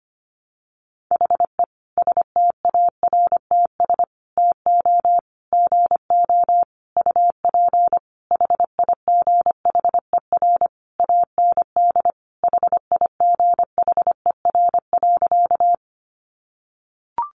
Morse code, heard as a keyed tone, about 25 wpm, 700 Hz. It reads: HE STARTS TO GO UP HIGHER AND HIGHER.
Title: Additional Easy Phrases 25wpm